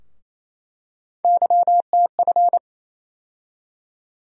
Morse code, heard as a keyed tone, about 28 wpm, 700 Hz. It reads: YTF
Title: USA Call Sign Postfixes - 3 Letters 28wpm